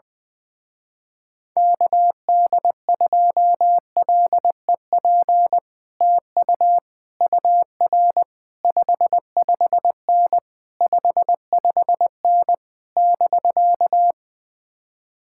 Morse code, heard as a keyed tone, about 20 wpm, 700 Hz. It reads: KD2LEP TU UR 55N 55N <BK>
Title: Speed-Racing Common QSO Phrases - 20wpm